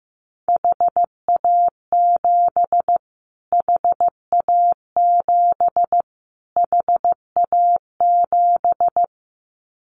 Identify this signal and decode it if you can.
Morse code, heard as a keyed tone, about 15 wpm, 700 Hz. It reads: HA7 HA7 HA7
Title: Callsign Prefixes for Top 25 DXCC - Repeated 3x - 15wpm